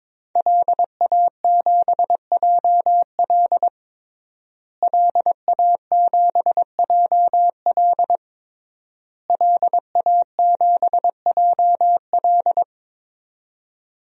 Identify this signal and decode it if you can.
Morse code, heard as a keyed tone, about 22 wpm, 700 Hz. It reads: LA7JL LA7JL LA7JL
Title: DXCC-100 Call Signs - 5 Characters - Repeated 3x - 22wpm